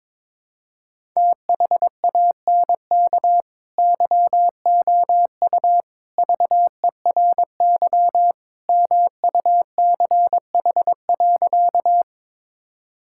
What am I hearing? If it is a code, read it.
Morse code, heard as a keyed tone, about 21 wpm, 700 Hz. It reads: THANK YOU VERY MUCH.
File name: practice / 22wpm / Sentences with 3-Word Phrases - 22wpm